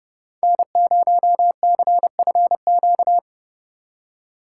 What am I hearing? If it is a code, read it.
Morse code, heard as a keyed tone, about 38 wpm, 700 Hz. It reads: N0CFQ